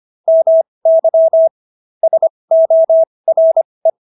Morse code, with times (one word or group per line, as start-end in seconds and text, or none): 0.3-1.5 MY
2.0-3.9 SORE